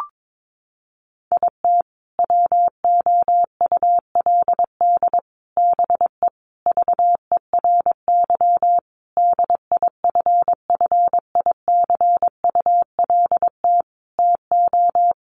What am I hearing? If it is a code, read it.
Morse code, heard as a keyed tone, about 22 wpm, 700 Hz. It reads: IT WOULD BE VERY DIFFICULT TO
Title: Sentences from Top 1100 Words 22wpm